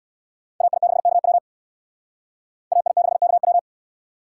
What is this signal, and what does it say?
Morse code, heard as a keyed tone, about 62 wpm, 700 Hz. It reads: AI6KW AI6KW
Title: Speed Racing - USA Call Signs 50wpm